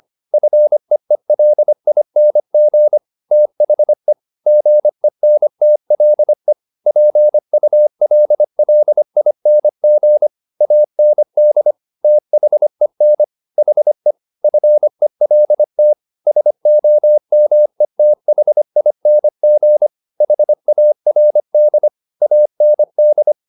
Morse code, heard as a keyed tone, about 25 wpm, 600 Hz. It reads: FEELING THE GENTLE PULLING AND THEN HE FELT SOMETHING HARD AND
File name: the old man and the sea part 1 25wpm